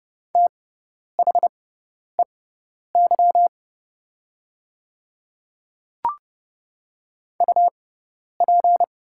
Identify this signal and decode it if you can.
Morse code, heard as a keyed tone, about 30 wpm, spaced as for 10 wpm, 700 Hz. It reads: THEY UP
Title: 6x Character Spacing - Top 500 Words - 30wpm